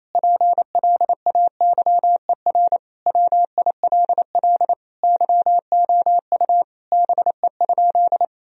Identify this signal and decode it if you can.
Morse code, heard as a keyed tone, about 28 wpm, 700 Hz. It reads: PLAYER WILL YOU BE?
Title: Sentences from Top 1000 Words - Review Part 4 - 28wpm